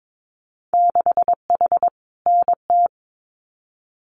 Morse code, heard as a keyed tone, about 22 wpm, 700 Hz. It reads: 6H NT
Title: Winter Field Day Exchanges Encore 22wpm